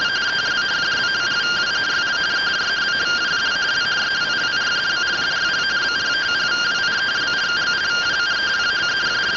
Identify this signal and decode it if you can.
Two-tone teleprinter signal, about 170 Hz shift.